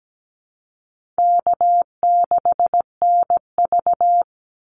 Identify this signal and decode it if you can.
Morse code, heard as a keyed tone, about 17 wpm, 700 Hz. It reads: K6NV
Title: Easy Call Signs - 17wpm